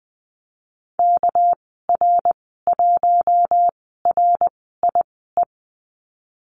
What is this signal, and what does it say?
Morse code, heard as a keyed tone, about 20 wpm, spaced as for 15 wpm, 700 Hz. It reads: KR1RIE
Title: 231 Lesson 37g - Letter x - Call Signs 20wpm